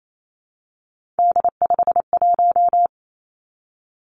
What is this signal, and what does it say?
Morse code, heard as a keyed tone, about 28 wpm, 700 Hz. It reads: D51
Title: Three Challenging Letter-Number Combinations 28wpm